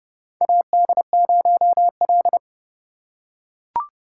Morse code, heard as a keyed tone, about 30 wpm, 700 Hz. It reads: AD0L
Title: USA Call Signs - 4 Characters - 2x Character Spacing - 30wpm